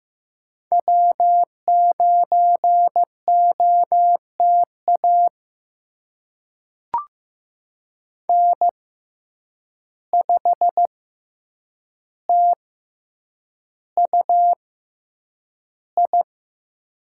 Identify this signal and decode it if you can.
Morse code, heard as a keyed tone, about 15 wpm, spaced as for 8 wpm, 700 Hz. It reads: W9OTA N5TUI
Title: USA Call Signs - 5 Characters - 6x Character Spacing - 15wpm